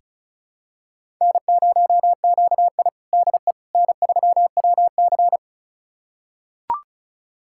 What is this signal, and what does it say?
Morse code, heard as a keyed tone, about 36 wpm, 700 Hz. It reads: N0QI DE N3WC
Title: USA Callsign DE Callsign - 4 Characters - 35wpm